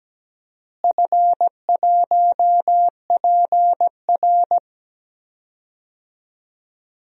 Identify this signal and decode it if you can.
Morse code, heard as a keyed tone, about 17 wpm, 700 Hz. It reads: F1PR
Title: DXCC-100 Call Signs - 4 Characters - Repeated 3x - 17wpm